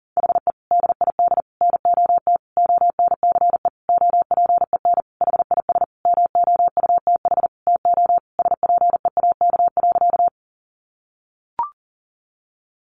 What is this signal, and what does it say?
Morse code, heard as a keyed tone, about 40 wpm, 700 Hz. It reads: HE DID NOT ONCE OPEN HIS MOUTH TO SPEAK.
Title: Sentences from Top 1500 Words 40wpm